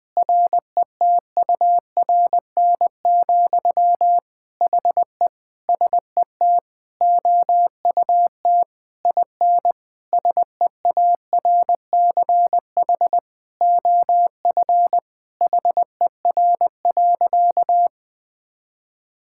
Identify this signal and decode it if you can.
Morse code, heard as a keyed tone, about 20 wpm, 700 Hz. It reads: RETURN, HE SET OUT IN SEARCH OF HER.
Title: Sentences from Top 1000 Words - Review Part 5 - 20wpm